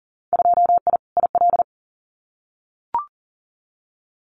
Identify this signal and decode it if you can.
Morse code, heard as a keyed tone, about 40 wpm, 700 Hz. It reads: 2I IL
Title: Winter Field Day Exchanges 40wpm